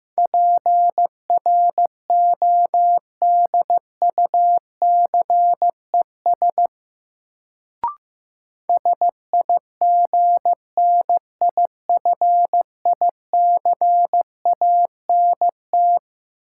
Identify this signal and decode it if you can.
Morse code, heard as a keyed tone, about 15 wpm, 700 Hz. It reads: PRODUCES SIGNIFICANT
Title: Sentences from Top 1000 Words - Review Part 6 - New Word Drill - 15wpm